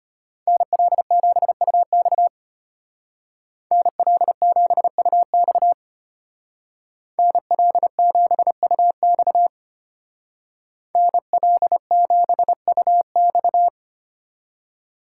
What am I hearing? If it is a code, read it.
Morse code, heard as a keyed tone, about 30 wpm, 700 Hz. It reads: NL7UX NL7UX NL7UX NL7UX
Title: Speed Racing - USA 2x2 Call Signs 25wpm